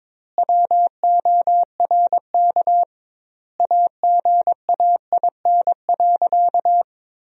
Morse code, heard as a keyed tone, about 22 wpm, 700 Hz. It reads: WORK AGAIN.
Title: 2x Word-Spacing - Sentences from Top 200 Words - 22wpm